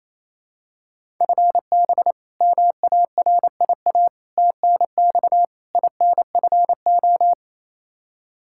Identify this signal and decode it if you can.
Morse code, heard as a keyed tone, about 28 wpm, 700 Hz. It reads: FB MARIA TNX INFO
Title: Practice QSO - Exchange 3 - More Info - 28wpm